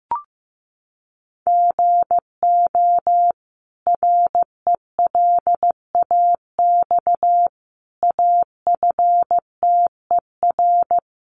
Morse code, heard as a keyed tone, about 15 wpm, 700 Hz. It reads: GO RELAX AFTER
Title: Sentences from Top 1400 Words 15wpm